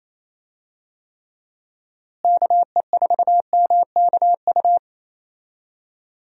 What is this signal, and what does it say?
Morse code, heard as a keyed tone, about 28 wpm, 700 Hz. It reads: KE4MKU
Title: USA Call Signs - 6 Characters - Repeated 3x - 28wpm